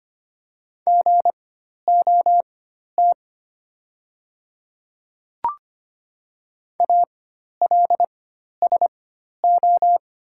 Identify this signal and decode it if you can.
Morse code, heard as a keyed tone, about 25 wpm, spaced as for 12 wpm, 700 Hz. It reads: GOT ALSO
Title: 4x Character Spacing - Top 500 Words - 25wpm